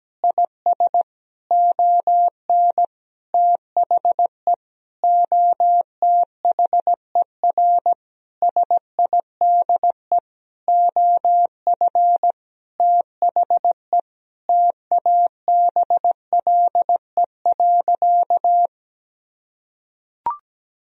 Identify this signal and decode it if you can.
Morse code, heard as a keyed tone, about 17 wpm, 700 Hz. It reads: IS ON THE OTHER SIDE OF THE TABLE.
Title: Sentences from Top 1000 Words - Review Part 4 - 17wpm